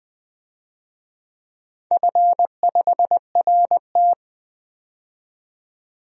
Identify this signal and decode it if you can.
Morse code, heard as a keyed tone, about 20 wpm, 700 Hz. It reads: F5RT
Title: DXCC-100 Call Signs - 4 Characters - 20wpm